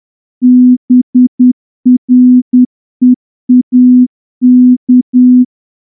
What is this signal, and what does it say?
Morse code, heard as a keyed tone, about 9 wpm, 250 Hz. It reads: BREAK
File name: morse-training-10WPM